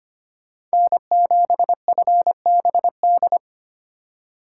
Morse code, heard as a keyed tone, about 25 wpm, 700 Hz. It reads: N7FBD
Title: USA Call Signs - 5 Characters - 3x Character Spacing - 25wpm